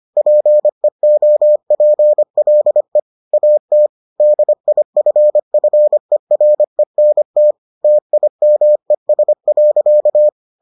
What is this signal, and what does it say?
Morse code, heard as a keyed tone, about 25 wpm, 600 Hz. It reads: PEOPLE AT DIFFERENT TIMES.